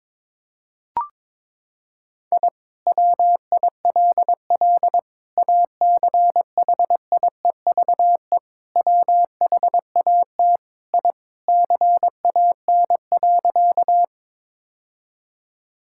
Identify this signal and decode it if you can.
Morse code, heard as a keyed tone, about 22 wpm, 700 Hz. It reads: I WILL ACHIEVE WHAT I CAN.
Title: Sentences from Top 1000 Words - Review Part 6 - 22wpm